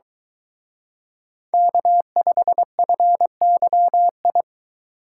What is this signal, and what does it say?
Morse code, heard as a keyed tone, about 24 wpm, 700 Hz. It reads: K5FYI